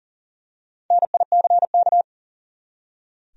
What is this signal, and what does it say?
Morse code, heard as a keyed tone, about 40 wpm, 700 Hz. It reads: NICK
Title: USA Common Names - 40wpm